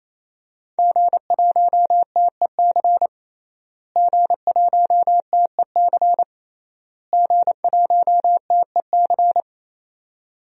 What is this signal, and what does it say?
Morse code, heard as a keyed tone, about 27 wpm, 700 Hz. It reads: G1TEC G1TEC G1TEC
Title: DXCC-100 Call Signs - 5 Characters - Repeated 3x - 28wpm